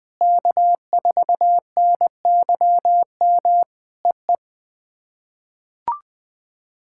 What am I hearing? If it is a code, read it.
Morse code, heard as a keyed tone, about 20 wpm, 700 Hz. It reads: K4NYM EE
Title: POTA QSO Practice - 20wpm